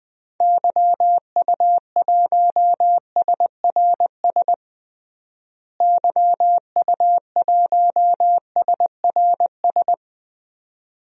Morse code, heard as a keyed tone, about 20 wpm, 700 Hz. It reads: YU1SRS YU1SRS
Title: DXCC-100 Call Signs - 6 Characters - Repeated 2x - 20wpm